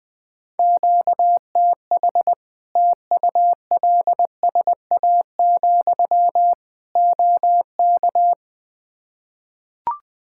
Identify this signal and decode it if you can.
Morse code, heard as a keyed tone, about 20 wpm, 700 Hz. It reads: QTH TULSA, OK
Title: Practice QSO - Exchange 1 - Beginning - 20wpm